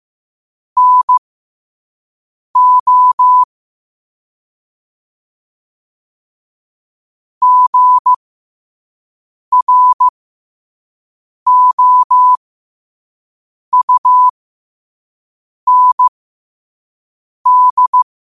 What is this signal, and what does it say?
Morse code, heard as a keyed tone, about 15 wpm, spaced as for 5 wpm, 1000 Hz. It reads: NO GROUND
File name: qotdfiles / 5wpm